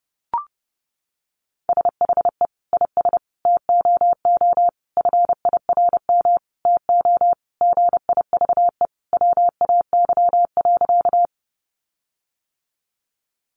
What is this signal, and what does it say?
Morse code, heard as a keyed tone, about 30 wpm, 700 Hz. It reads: SHE IS TOO FIRM TO GIVE WAY.
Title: Sentences from Top 1400 Words 30wpm